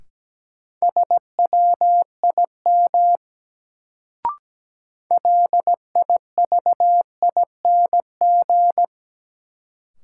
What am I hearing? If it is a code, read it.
Morse code, heard as a keyed tone, about 17 wpm, 700 Hz. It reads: SWIM LIVING
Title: Sentences from Top 1000 Words - Review Part 6 - New Word Drill - 17wpm